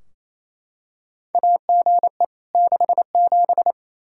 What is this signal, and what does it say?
Morse code, heard as a keyed tone, about 27 wpm, 700 Hz. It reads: AGE 67
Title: Practice QSO - Exchange 3 - More Info - 28wpm